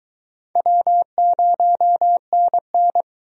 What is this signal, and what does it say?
Morse code, heard as a keyed tone, about 24 wpm, 700 Hz. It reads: W0NN